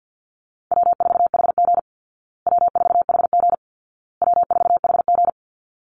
Morse code, heard as a keyed tone, about 48 wpm, 700 Hz. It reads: W4HG W4HG W4HG
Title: Easy Call Signs Repeated 3x - 50wpm